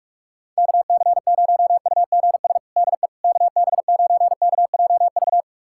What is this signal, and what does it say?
Morse code, heard as a keyed tone, about 41 wpm, 700 Hz. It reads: KC0UGS DE KB9KJV
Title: USA Callsign DE Callsign - 6 Characters - 45wpm